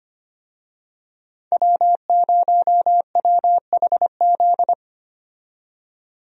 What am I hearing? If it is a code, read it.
Morse code, heard as a keyed tone, about 25 wpm, 700 Hz. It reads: W0WHZ
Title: USA Call Signs - 5 Characters - 25wpm